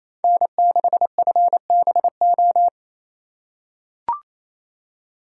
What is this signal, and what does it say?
Morse code, heard as a keyed tone, about 28 wpm, 700 Hz. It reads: N6FBO
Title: USA Call Signs - 5 Characters - 4x Character Spacing - 28wpm